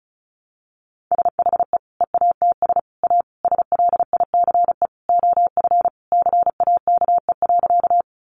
Morse code, heard as a keyed tone, about 34 wpm, 700 Hz. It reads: SHE EATS A SLICE OF CAKE.
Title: Additional Easy Phrases 35wpm